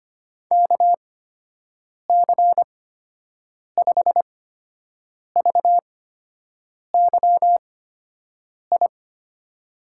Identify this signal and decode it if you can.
Morse code, heard as a keyed tone, about 25 wpm, spaced as for 7 wpm, 700 Hz. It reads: KC5VYI
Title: USA Call Signs - 6 Characters - 8x Character Spacing - 25wpm